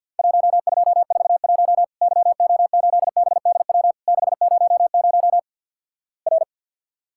Morse code, heard as a keyed tone, about 50 wpm, 700 Hz. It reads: 1241 YO8BDW 600 R
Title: Worked All Europe DX Contest - QTC Exchange - 50wpm